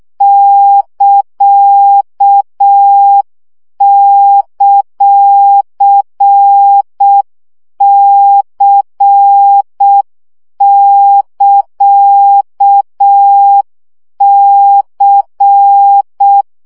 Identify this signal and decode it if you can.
Morse code, heard as a keyed tone, about 6 wpm, 800 Hz. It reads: <KA>;C<KA>C
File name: CW gyakorlas c ct pontosvesszo 5 baud